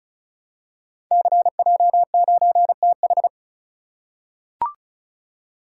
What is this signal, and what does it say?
Morse code, heard as a keyed tone, about 34 wpm, 690 Hz. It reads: CJ9TH
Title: Code Groups - 5 Letter:Numbers - 35wpm